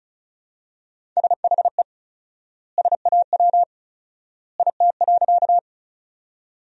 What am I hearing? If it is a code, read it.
Morse code, heard as a keyed tone, about 36 wpm, 700 Hz. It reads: SHE SAW IT.
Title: 4x Word-Spacing - Sentences from Top 300 Words - 35wpm